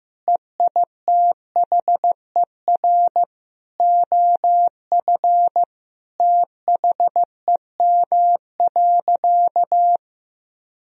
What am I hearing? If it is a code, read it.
Morse code, heard as a keyed tone, about 16 wpm, 700 Hz. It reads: EITHER OF THEM.
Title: Sentences from Top 1000 Words - Review Part 4 - 15wpm